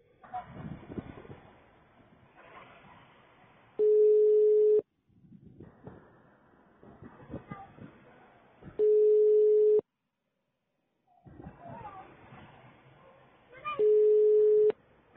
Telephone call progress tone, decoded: ringback tone